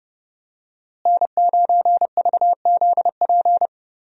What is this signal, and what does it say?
Morse code, heard as a keyed tone, about 29 wpm, 700 Hz. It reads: N9VZP
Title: USA Call Signs - 5 Characters - Repeated 2x - 30wpm